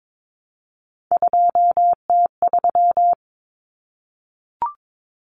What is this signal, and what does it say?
Morse code, heard as a keyed tone, about 22 wpm, 700 Hz. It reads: 2T3
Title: Common Cut Numbers - 3-Digit 22wpm